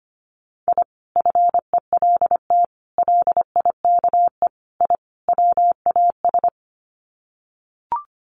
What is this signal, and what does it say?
Morse code, heard as a keyed tone, about 25 wpm, 690 Hz. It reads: I FELT LIKE I WAS
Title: Sets of 5 Words Encore 25wpm